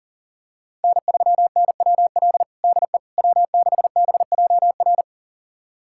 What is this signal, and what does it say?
Morse code, heard as a keyed tone, about 40 wpm, 700 Hz. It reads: N3NWL DE W6BJR
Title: USA Callsign DE Callsign - 5 Characters - 40wpm